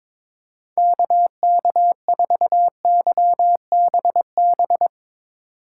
Morse code, heard as a keyed tone, about 24 wpm, 700 Hz. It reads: KK4YBB
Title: Speed Racing - USA 2x3 Call Signs 22wpm